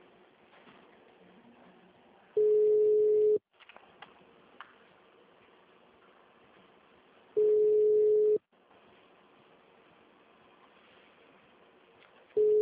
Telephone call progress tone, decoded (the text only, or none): ringback tone